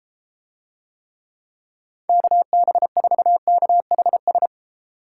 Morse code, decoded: KB4KHS

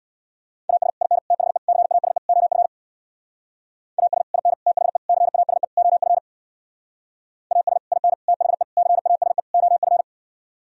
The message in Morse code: AS IN THE CASE OF AS IN THE CASE OF AS IN THE CASE OF

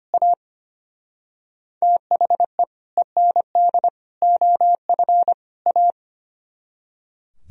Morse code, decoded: A THE END OF A

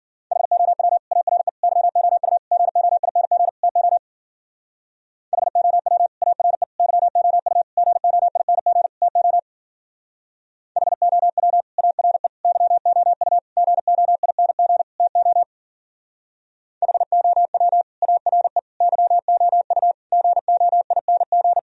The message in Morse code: HOW ARE YOU GOING TO HOW ARE YOU GOING TO HOW ARE YOU GOING TO HOW ARE YOU GOING

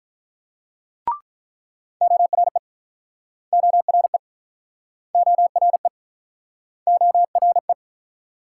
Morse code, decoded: ORE ORE ORE ORE

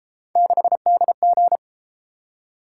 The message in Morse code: 6DG

